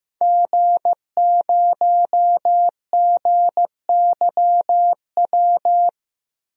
Morse code, heard as a keyed tone, about 15 wpm, 700 Hz. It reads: G0GYW
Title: DXCC-100 Call Signs - 5 Characters - 15wpm